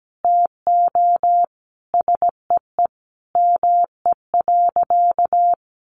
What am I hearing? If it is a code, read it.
Morse code, heard as a keyed tone, about 17 wpm, 700 Hz. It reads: TO SEE ME.
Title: Sentences from Top 1500 Words 17wpm